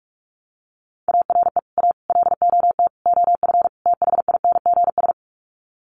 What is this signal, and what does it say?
Morse code, heard as a keyed tone, about 44 wpm, 700 Hz. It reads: ARE A LOT OF THINGS
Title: Sets of 5 Words Encore2 45wpm